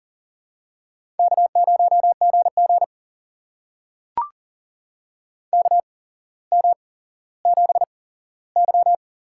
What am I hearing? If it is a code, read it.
Morse code, heard as a keyed tone, about 38 wpm, spaced as for 17 wpm, 700 Hz. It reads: K0GG KM7Y
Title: USA Call Signs - 4 Characters - 8x Character Spacing - 40wpm